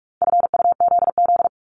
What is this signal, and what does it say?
Morse code, heard as a keyed tone, about 46 wpm, 700 Hz. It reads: FUZZ
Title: Multi-letter Phonemes 45wpm